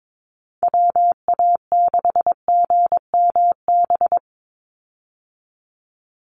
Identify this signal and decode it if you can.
Morse code, heard as a keyed tone, about 22 wpm, 700 Hz. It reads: WA6GMB
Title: USA Call Signs 22wpm